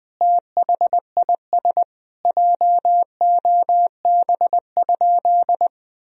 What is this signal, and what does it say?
Morse code, heard as a keyed tone, about 20 wpm, 700 Hz. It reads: THIS JOB?
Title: Sentences from Top 1000 Words - Review Part 3 - 20wpm